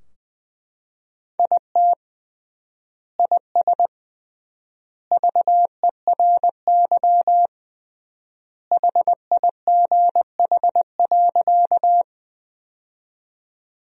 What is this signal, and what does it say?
Morse code, heard as a keyed tone, about 21 wpm, 700 Hz. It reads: IT IS VERY HIGH.
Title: 3x Word-Spacing - Sentences from Top 200 Words - 20wpm